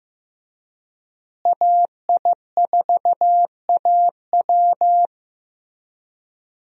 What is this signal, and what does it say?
Morse code, heard as a keyed tone, about 16 wpm, 700 Hz. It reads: AI4AW
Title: USA Call Signs - 5 Characters - Repeated 3x - 15wpm